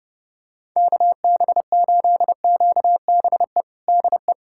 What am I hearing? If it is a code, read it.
Morse code, heard as a keyed tone, about 32 wpm, 700 Hz. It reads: KB8QBE DE
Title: USA Callsign DE Callsign 30wpm